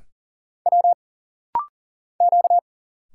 Morse code, heard as a keyed tone, about 40 wpm, 700 Hz. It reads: W Q
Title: Single Letters - Rapid-Fire 40wpm